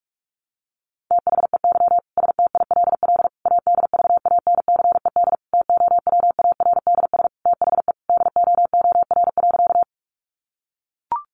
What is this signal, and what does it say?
Morse code, heard as a keyed tone, about 44 wpm, 700 Hz. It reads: THEY STILL ADVANCED TOWARDS THE DOOR.
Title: Sentences from Top 1200 Words 45wpm